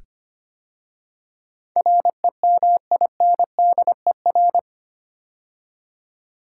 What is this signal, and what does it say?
Morse code, heard as a keyed tone, about 26 wpm, 700 Hz. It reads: REMINDER
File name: 8 Letter Words - First 3 Letters Given - 25wpm